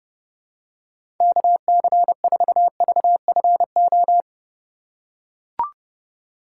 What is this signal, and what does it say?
Morse code, heard as a keyed tone, about 30 wpm, 700 Hz. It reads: KC4VFO
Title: USA Call Signs - 6 Characters - 4x Character Spacing - 30wpm